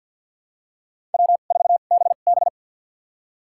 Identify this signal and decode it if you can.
Morse code, heard as a keyed tone, about 46 wpm, spaced as for 35 wpm, 700 Hz. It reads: W4BB
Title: USA Call Signs - 4 Characters - 2x Character Spacing - 50wpm